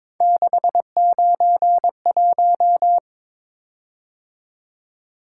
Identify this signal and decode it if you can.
Morse code, heard as a keyed tone, about 22 wpm, 700 Hz. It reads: 691
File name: practice / 22wpm / Three Letter-Number Combinations - 22wpm